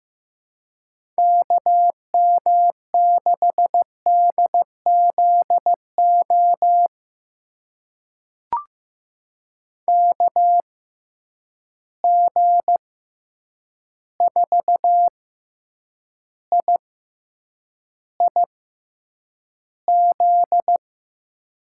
Morse code, heard as a keyed tone, about 15 wpm, spaced as for 8 wpm, 700 Hz. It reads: KM6DZO KG4IIZ
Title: USA Call Signs - 6 Characters - 6x Character Spacing - 15wpm